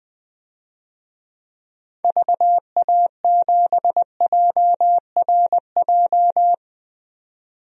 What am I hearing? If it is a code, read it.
Morse code, heard as a keyed tone, about 20 wpm, 700 Hz. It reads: VA7JRJ
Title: DXCC-100 Call Signs - 6 Characters - 20wpm